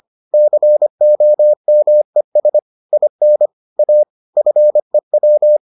Morse code, heard as a keyed tone, about 25 wpm, 600 Hz. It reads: COMES IN A FEW